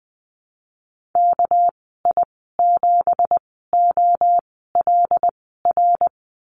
Morse code, heard as a keyed tone, about 20 wpm, spaced as for 14 wpm, 700 Hz. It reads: KI7OLR
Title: 200 Lesson 32g - Letter k - Call Signs 20wpm